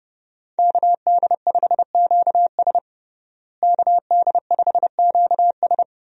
Morse code, heard as a keyed tone, about 32 wpm, 700 Hz. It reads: KD5QS KD5QS
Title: USA Call Signs - 5 Characters - Repeated 2x - 30wpm